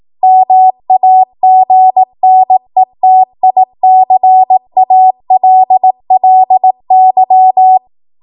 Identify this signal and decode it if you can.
Morse code, heard as a keyed tone, about 18 wpm, 750 Hz. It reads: MAGNETICALLY
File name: Morse / 18WPM